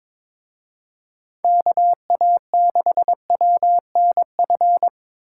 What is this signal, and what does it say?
Morse code, heard as a keyed tone, about 25 wpm, 700 Hz. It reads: KA6WNF
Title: Speed Racing - USA 2x3 Call Signs 22wpm